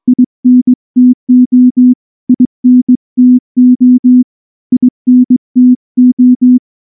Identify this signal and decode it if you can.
Morse code, heard as a keyed tone, about 19 wpm, 250 Hz. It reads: INTO INTO INTO
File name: morse / morse-training-20WPM-6